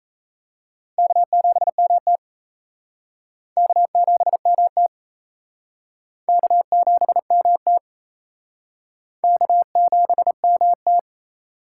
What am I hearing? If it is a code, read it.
Morse code, heard as a keyed tone, about 34 wpm, 700 Hz. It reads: K7MT K7MT K7MT K7MT
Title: Speed Racing - USA 1x2 Call Signs 28wpm